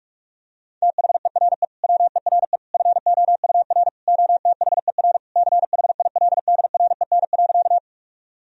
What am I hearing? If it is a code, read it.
Morse code, heard as a keyed tone, about 48 wpm, 700 Hz. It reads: THERE WERE FOUR OTHER CHILDREN.
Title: Sentences from Top 1000 Words - Review Part 1 - 45wpm